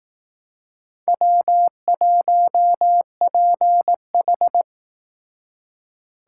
Morse code, heard as a keyed tone, about 18 wpm, 700 Hz. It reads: W1PH
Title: Speed Racing - USA 1x2 Call Signs 15wpm